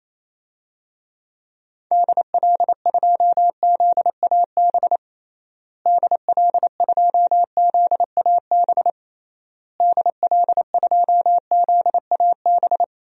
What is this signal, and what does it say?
Morse code, heard as a keyed tone, about 28 wpm, 700 Hz. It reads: DL2ZAB DL2ZAB DL2ZAB
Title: DXCC-100 Call Signs - 6 Characters - Repeated 3x - 28wpm